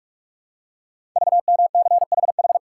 Morse code, heard as a keyed tone, about 44 wpm, 700 Hz. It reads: VMCHH
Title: Code Groups - 5 Letter:Numbers - 45wpm